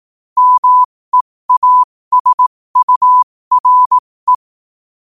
Morse code, heard as a keyed tone, about 18 wpm, spaced as for 15 wpm, 1000 Hz. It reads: MEASURE
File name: qotdfiles / 15wpm